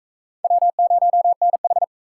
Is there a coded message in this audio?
Yes: Morse code from 0.4 to 1.8 s, heard as a keyed tone, about 40 wpm, 700 Hz. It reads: W0NH